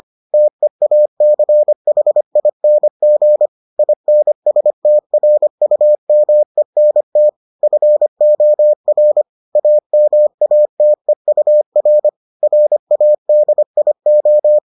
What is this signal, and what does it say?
Morse code, heard as a keyed tone, about 25 wpm, 600 Hz. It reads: TEACHING INSTRUMENT FOR AMATEUR RADIO